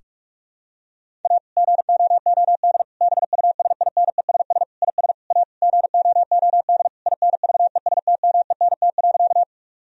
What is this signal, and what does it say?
Morse code, heard as a keyed tone, about 42 wpm, 700 Hz. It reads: A GOOD BUSINESS IS A GOOD INVESTMENT.
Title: Sentences from Top 1000 Words - Review Part 4 - 45wpm